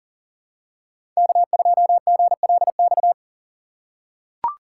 Morse code, heard as a keyed tone, about 40 wpm, 700 Hz. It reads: K2GLX
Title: USA Call Signs - 5 Characters - 2x Character Spacing - 40wpm